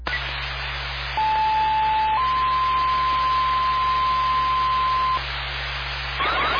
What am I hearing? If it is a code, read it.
Two-tone page, heard as a one-second tone, then a three-second tone, about 870 then 1040 Hz, with noise. It is heard from 1.2 to 5.2 s.